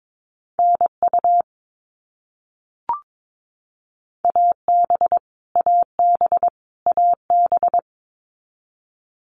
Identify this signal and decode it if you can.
Morse code, heard as a keyed tone, about 22 wpm, 700 Hz. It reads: NU AB AB AB
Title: Canadian Provinces and Territories - Repeated 3x - 22wpm